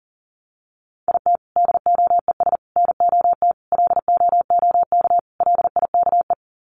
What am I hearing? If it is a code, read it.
Morse code, heard as a keyed tone, about 38 wpm, 700 Hz. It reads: IT DOES NOT LOOK LIKE
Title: Sets of 5 Words Encore2 40wpm